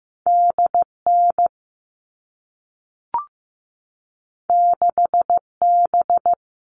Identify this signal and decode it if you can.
Morse code, heard as a keyed tone, about 15 wpm, 700 Hz. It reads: DN 6B
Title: All Cut Numbers - 2-Digit 15wpm